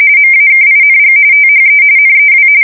RTTY, about 45.45 baud. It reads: MAKING THE NEW
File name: fsk170 / USS-New-Jersey